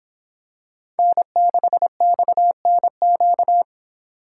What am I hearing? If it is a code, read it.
Morse code, heard as a keyed tone, about 24 wpm, 700 Hz. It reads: N6XNQ